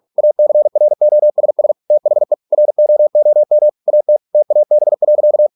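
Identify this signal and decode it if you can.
Morse code, heard as a keyed tone, about 48 wpm, 600 Hz. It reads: ACROSS THE ROOM AT TAB.
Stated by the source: Morse practice at 45 wpm, the novel Never by Ken Follett in English, from CW med Gnister.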